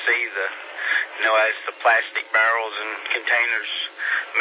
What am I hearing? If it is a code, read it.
Emergency Alert System (SAME) tones, heard as harsh buzzing bursts, with speech. It is heard from 0.7 to 3.7 s.